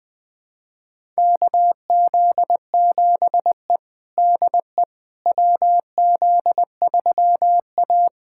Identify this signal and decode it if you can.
Morse code, heard as a keyed tone, about 20 wpm, 700 Hz. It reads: KZ7E DE WZ3A
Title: USA Callsign DE Callsign - 4 Characters - 20wpm